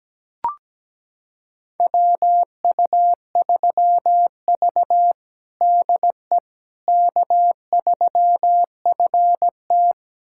Morse code, heard as a keyed tone, about 17 wpm, 700 Hz. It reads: WU3V DE K3FT
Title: USA Callsign DE Callsign - 4 Characters - 17wpm